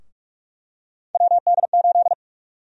Morse code, heard as a keyed tone, about 48 wpm, 700 Hz. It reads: WD8
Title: USA Call Sign Prefixes - Repeated 2x - 45wpm